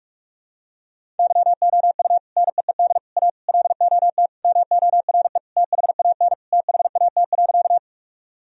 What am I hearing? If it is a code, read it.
Morse code, heard as a keyed tone, about 43 wpm, 700 Hz. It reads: YOU NEED A LOT MORE THAN THAT.
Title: Sentences from Top 1000 Words - Review Part 3 - 45wpm